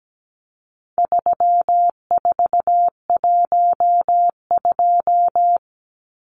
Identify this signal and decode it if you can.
Morse code, heard as a keyed tone, about 17 wpm, 700 Hz. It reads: 3412
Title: Numbers - 4-Digit 17wpm